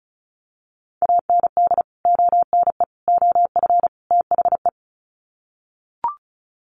Morse code, heard as a keyed tone, about 34 wpm, 700 Hz. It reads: AND ONE OF THE
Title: Sets of 4 Words 35wpm